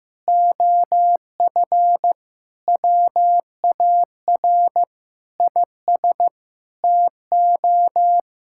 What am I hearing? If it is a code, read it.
Morse code, heard as a keyed tone, about 14 wpm, 700 Hz. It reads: OF WAR IS TO
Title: Sentences from Top 1600 Words 15wpm